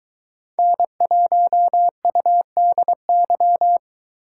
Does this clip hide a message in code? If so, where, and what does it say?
Yes: Morse code from 0.6 to 3.8 s, heard as a keyed tone, about 22 wpm, 700 Hz. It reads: N1UDY